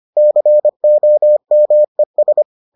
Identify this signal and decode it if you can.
Morse code, heard as a keyed tone, about 25 wpm, 600 Hz. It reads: COMES